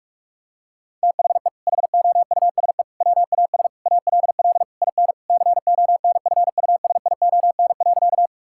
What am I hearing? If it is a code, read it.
Morse code, heard as a keyed tone, about 41 wpm, 700 Hz. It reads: THE HOUSE WAS ALL IN CONFUSION.
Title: Sentences from Top 1600 Words 45wpm